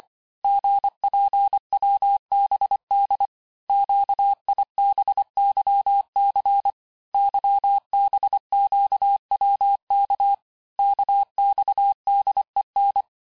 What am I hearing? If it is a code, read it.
Morse code, heard as a keyed tone, about 25 wpm, 800 Hz. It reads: GPWBD QIBYC YBQWK KXDEN
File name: texte-chiffre-vitesse-23mpm